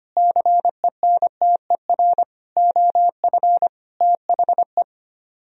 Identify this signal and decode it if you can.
Morse code, heard as a keyed tone, about 31 wpm, 700 Hz. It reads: CENTER OF THE
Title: Speed-Racing - Sets of 5 Words 25wpm